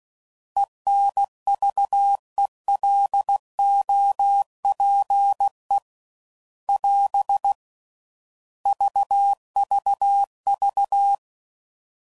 Morse code, heard as a keyed tone, about 16 wpm, 800 Hz. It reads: ENVELOPE <AS> VVV